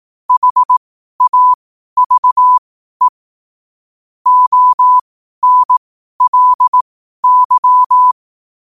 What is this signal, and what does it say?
Morse code, heard as a keyed tone, about 18 wpm, spaced as for 12 wpm, 1000 Hz. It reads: HAVE ONLY
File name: qotdfiles / 12wpm